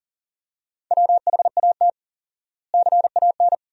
Morse code, heard as a keyed tone, about 40 wpm, 700 Hz. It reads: WHAT CAN
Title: 4x Word-Spacing - Sentences from Top 100 Words - 40wpm